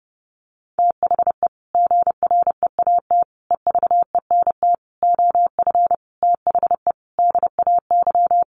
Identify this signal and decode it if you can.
Morse code, heard as a keyed tone, about 28 wpm, 700 Hz. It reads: THE GREAT EVENT OF THE DAY
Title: Sentences from Top 1500 Words 30wpm